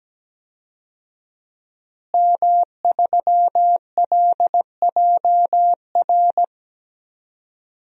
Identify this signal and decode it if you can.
Morse code, heard as a keyed tone, about 17 wpm, 700 Hz. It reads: M3LJR
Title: DXCC-100 Call Signs - 5 Characters - Repeated 3x - 17wpm